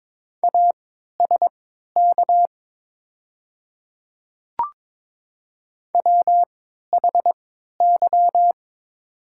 Morse code, heard as a keyed tone, about 22 wpm, spaced as for 12 wpm, 700 Hz. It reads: ASK WHY
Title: 3x Character Spacing - Top 200 Words - 22wpm